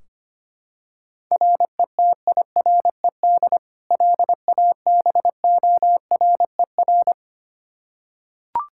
Morse code, heard as a keyed tone, about 25 wpm, 700 Hz. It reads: RETIRED LABORER
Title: K1USN SST QSO Practice 25wpm